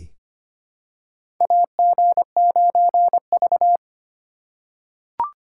Morse code, heard as a keyed tone, about 25 wpm, 700 Hz. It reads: AG9V
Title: USA Call Signs - 2x Character Spacing - 25wpm